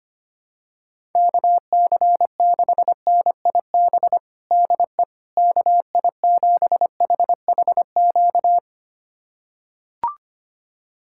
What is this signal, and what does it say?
Morse code, heard as a keyed tone, about 25 wpm, 700 Hz. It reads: KC6NIB DE KI7HHQ
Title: USA Callsign DE Callsign - 6 Characters - 25wpm